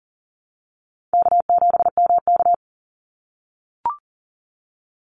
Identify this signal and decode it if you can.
Morse code, heard as a keyed tone, about 38 wpm, 700 Hz. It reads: K7MK
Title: NA SOTA Activator Callsigns 40wpm